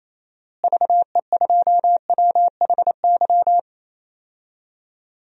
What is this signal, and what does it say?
Morse code, heard as a keyed tone, about 27 wpm, 700 Hz. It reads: VE2WHY
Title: DXCC-100 Call Signs - 6 Characters - Repeated 3x - 28wpm